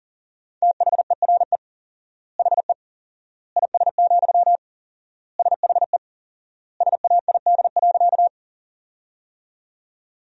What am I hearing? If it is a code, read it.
Morse code, heard as a keyed tone, about 40 wpm, 700 Hz. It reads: THERE HE IS, SHE SAID.
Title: 4x Word-Spacing - Sentences from Top 100 Words - 40wpm